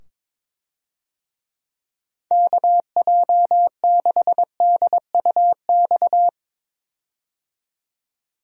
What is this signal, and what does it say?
Morse code, heard as a keyed tone, about 22 wpm, 700 Hz. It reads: KJ6DUX
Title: USA Call Signs - 6 Characters - 22wpm